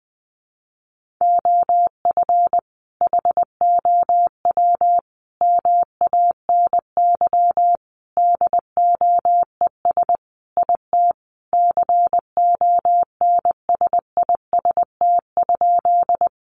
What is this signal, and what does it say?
Morse code, heard as a keyed tone, about 19 wpm, 700 Hz. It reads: OF HOW MANY DOES IT CONSIST?
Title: Sentences from Top 1400 Words 20wpm